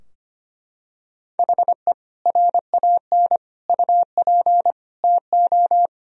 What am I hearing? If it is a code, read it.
Morse code, heard as a keyed tone, about 25 wpm, 700 Hz. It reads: HE RAN UP TO
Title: Sentences from Top 1000 Words - Review Part 3 - 25wpm